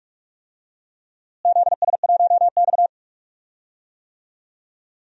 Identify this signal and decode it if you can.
Morse code, heard as a keyed tone, about 42 wpm, 700 Hz. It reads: ZS1X
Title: DXCC-100 Call Signs - 4 Characters - Repeated 2x - 45wpm